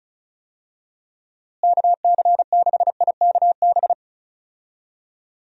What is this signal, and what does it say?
Morse code, heard as a keyed tone, about 36 wpm, 700 Hz. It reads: KC6IKB
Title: USA Call Signs - 6 Characters - Repeated 3x - 35wpm